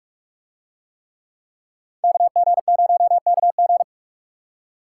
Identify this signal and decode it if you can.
Morse code, heard as a keyed tone, about 44 wpm, 700 Hz. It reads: KG0KG